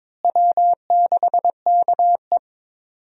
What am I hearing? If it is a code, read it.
Morse code, heard as a keyed tone, about 21 wpm, 700 Hz. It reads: W6KE
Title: USA Call Signs - 4 Characters - Repeated 3x - 22wpm